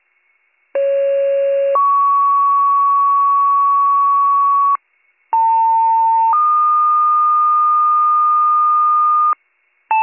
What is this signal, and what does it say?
Two-tone page, heard as a one-second tone, then a three-second tone, about 550 then 1120 Hz, about 900 then 1220 Hz. It is heard from 0.7 to 9.3 s.